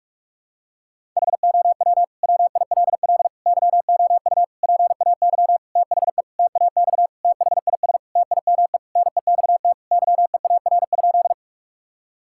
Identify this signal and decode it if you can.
Morse code, heard as a keyed tone, about 46 wpm, 700 Hz. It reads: HOW WILL YOU PAY THE TAX THIS TIME NEXT YEAR?
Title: Sentences from Top 1000 Words - Review Part 5 - 45wpm